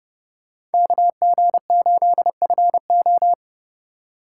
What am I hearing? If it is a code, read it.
Morse code, heard as a keyed tone, about 30 wpm, 700 Hz. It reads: KG8FO
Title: USA Call Signs - 5 Characters - 3x Character Spacing - 30wpm